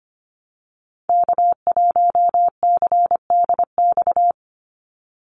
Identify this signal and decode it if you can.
Morse code, heard as a keyed tone, about 25 wpm, 700 Hz. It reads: K1CDX
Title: Call Signs - 25wpm